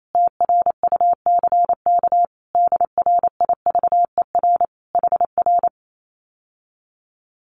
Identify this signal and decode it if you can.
Morse code, heard as a keyed tone, about 28 wpm, 700 Hz. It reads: TRUCK DRIVER HR
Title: Common QSO Phrases Encore 28wpm